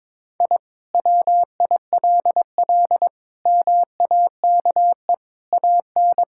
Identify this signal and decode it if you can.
Morse code, heard as a keyed tone, about 22 wpm, 700 Hz. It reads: I WILL MAKE AN
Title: Sentences from Top 1000 Words - Review Part 5 - 22wpm